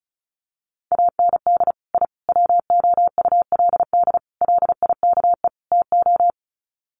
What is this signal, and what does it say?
Morse code, heard as a keyed tone, about 34 wpm, 700 Hz. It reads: AND I WOULD LIKE TO
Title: Sets of 5 Words Encore 35wpm